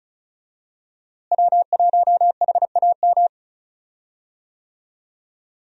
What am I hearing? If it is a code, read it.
Morse code, heard as a keyed tone, about 35 wpm, 700 Hz. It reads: W1HAM
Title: USA Call Signs - 5 Characters - Repeated 2x - 35wpm